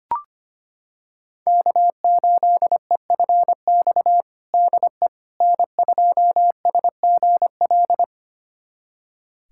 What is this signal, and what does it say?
Morse code, heard as a keyed tone, about 25 wpm, 700 Hz. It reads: K8EFX DE N2SGL
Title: USA Callsign DE Callsign - 5 Characters - 25wpm